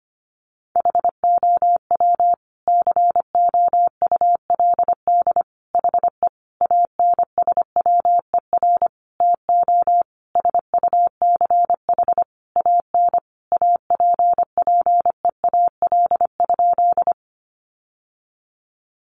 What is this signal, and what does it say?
Morse code, heard as a keyed tone, about 25 wpm, 700 Hz. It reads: HOW COULD HE ANSWER TO SUCH AN APPEAL?
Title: Sentences from Top 1300 Words 25wpm